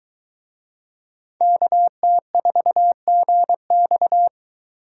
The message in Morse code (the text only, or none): KT4GX